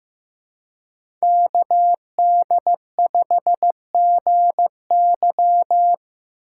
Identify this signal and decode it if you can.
Morse code, heard as a keyed tone, about 19 wpm, 700 Hz. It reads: KD5GY
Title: Speed Racing - USA 2x2 Call Signs 15wpm